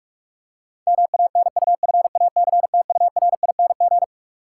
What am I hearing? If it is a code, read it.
Morse code, heard as a keyed tone, about 46 wpm, 700 Hz. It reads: MANUFACTURING
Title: Practice QSO - Exchange 3 - More Info - 45wpm